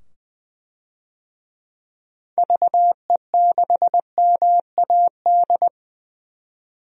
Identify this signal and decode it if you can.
Morse code, heard as a keyed tone, about 20 wpm, 700 Hz. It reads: VE6MAD
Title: DXCC-100 Call Signs - 6 Characters - 20wpm